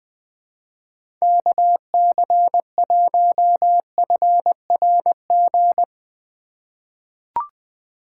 Morse code, heard as a keyed tone, about 20 wpm, 700 Hz. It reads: KC1FRG
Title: USA Call Signs - 6 Characters - 4x Character Spacing - 20wpm